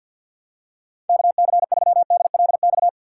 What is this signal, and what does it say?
Morse code, heard as a keyed tone, about 52 wpm, 700 Hz. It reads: KC3DLX
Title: USA Call Signs - 6 Characters - 3x Character Spacing - 50wpm